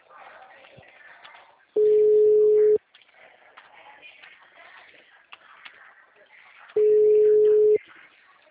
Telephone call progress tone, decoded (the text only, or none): ringback tone